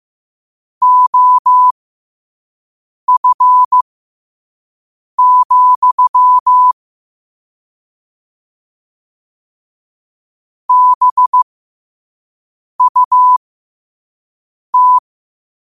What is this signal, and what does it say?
Morse code, heard as a keyed tone, about 15 wpm, spaced as for 5 wpm, 1000 Hz. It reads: OF, BUT
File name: qotdfiles / 5wpm